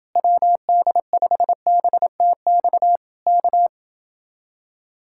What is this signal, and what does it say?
Morse code, heard as a keyed tone, about 29 wpm, 700 Hz. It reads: WD5BTX K